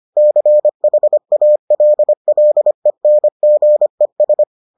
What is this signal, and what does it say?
Morse code, heard as a keyed tone, about 25 wpm, 600 Hz. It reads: CHALLENGES